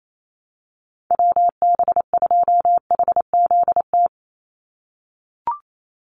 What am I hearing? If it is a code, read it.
Morse code, heard as a keyed tone, about 28 wpm, 700 Hz. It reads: WB2HZT
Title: USA Call Signs 28wpm